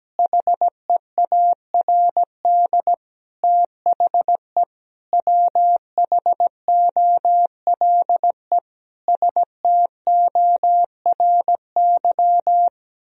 Morse code, heard as a keyed tone, about 17 wpm, 700 Hz. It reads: HEARD THE WHOLE STORY
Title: Sentences from Top 1000 Words - Review Part 4 - 17wpm